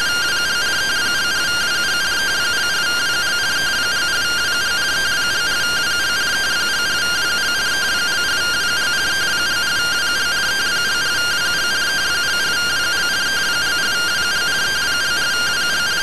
Two-tone teleprinter signal, about 170 Hz shift.